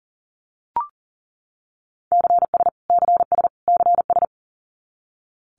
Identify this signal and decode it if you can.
Morse code, heard as a keyed tone, about 40 wpm, 700 Hz. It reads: CS CS CS
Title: Abbreviations Repeated 3x - 40wpm